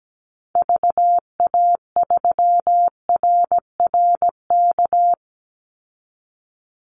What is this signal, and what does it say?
Morse code, heard as a keyed tone, about 17 wpm, 700 Hz. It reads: VA3RRK
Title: Call Signs Repeated 3x - 17wpm